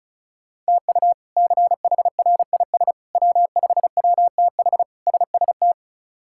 Morse code, heard as a keyed tone, about 36 wpm, 700 Hz. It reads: TU CHRIS W5WTH SST
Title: K1USN SST QSO Practice - 35wpm